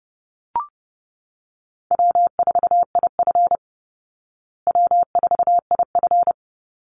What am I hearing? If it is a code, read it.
Morse code, heard as a keyed tone, about 30 wpm, 700 Hz. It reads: W4IF W4IF
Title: Call Signs Repeated 2x - 30wpm